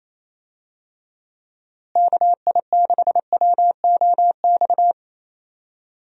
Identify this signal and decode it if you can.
Morse code, heard as a keyed tone, about 28 wpm, 700 Hz. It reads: KI6WOX
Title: USA Call Signs - 6 Characters - Repeated 2x - 28wpm